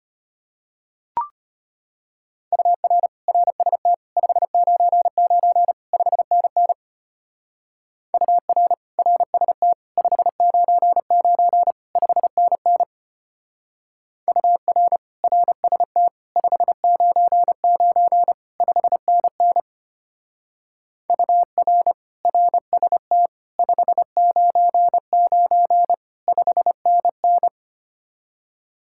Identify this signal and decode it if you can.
Morse code, heard as a keyed tone, about 30 wpm, 700 Hz. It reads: UR RST 599 5NN UR RST 599 5NN UR RST 599 5NN UR RST 599 5NN
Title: Speed-Racing Common QSO Phrases - 25wpm